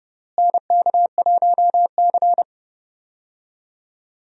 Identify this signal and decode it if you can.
Morse code, heard as a keyed tone, about 30 wpm, 700 Hz. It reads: NK1C
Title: USA Call Signs - 4 Characters - Repeated 2x - 30wpm